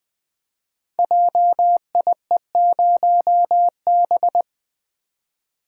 Morse code, heard as a keyed tone, about 20 wpm, 700 Hz. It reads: JIE0B
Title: Code Groups - 5 Letter:Numbers - 20wpm